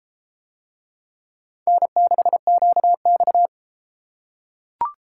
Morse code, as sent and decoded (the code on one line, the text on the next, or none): -. -.... --.- -..-
N6QX